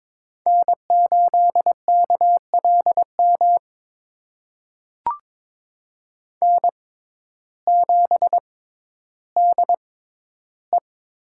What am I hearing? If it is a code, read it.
Morse code, heard as a keyed tone, about 21 wpm, spaced as for 12 wpm, 700 Hz. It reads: N8KLM N7DE
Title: USA Call Signs - 5 Characters - 6x Character Spacing - 22wpm